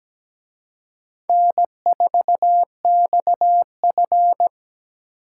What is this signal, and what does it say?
Morse code, heard as a keyed tone, about 20 wpm, 700 Hz. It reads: N4XF
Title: Speed Racing - USA 1x2 Call Signs 17wpm